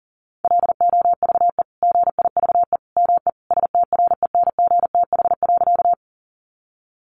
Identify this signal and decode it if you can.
Morse code, heard as a keyed tone, about 40 wpm, 700 Hz. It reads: LOVE GIVE ME STRENGTH.
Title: Sentences from Top 1300 Words 40wpm